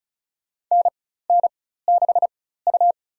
Morse code, heard as a keyed tone, about 36 wpm, spaced as for 16 wpm, 700 Hz. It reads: NN6U
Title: USA Call Signs - 4 Characters - 4x Character Spacing - 35wpm